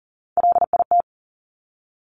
Morse code, heard as a keyed tone, about 40 wpm, 700 Hz. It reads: LIT
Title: Top Three Letter Combinations in Words 40wpm